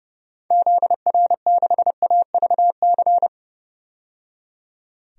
Morse code, heard as a keyed tone, about 30 wpm, 700 Hz. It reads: ZR6AVC
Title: DXCC-100 Call Signs - 6 Characters - 30wpm